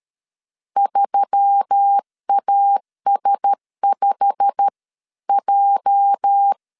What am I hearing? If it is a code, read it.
Morse code, heard as a keyed tone, about 13 wpm, 800 Hz. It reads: 3AS5J
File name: CW / DET / 1 9 1500C S B 12wpm 3819